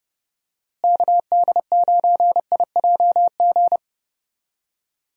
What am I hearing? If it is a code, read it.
Morse code, heard as a keyed tone, about 30 wpm, 700 Hz. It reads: KD9IJG
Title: USA Call Signs - 6 Characters - Repeated 3x - 30wpm